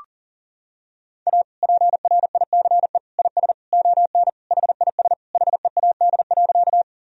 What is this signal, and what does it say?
Morse code, heard as a keyed tone, about 40 wpm, 700 Hz. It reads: A PRICE IS ON HIS HEAD.
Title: Sentences from Top 1000 Words - Review Part 4 - 40wpm